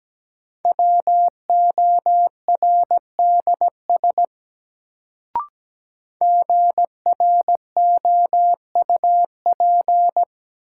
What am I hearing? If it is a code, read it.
Morse code, heard as a keyed tone, about 17 wpm, 700 Hz. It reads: WORDS GROUP
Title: Sets of 3 Words - New Word Drill - 17wpm